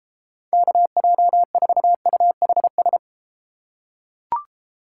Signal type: Morse code

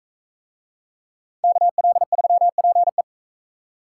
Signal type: Morse code

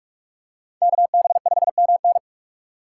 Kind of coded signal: Morse code

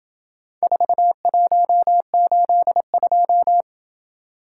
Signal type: Morse code

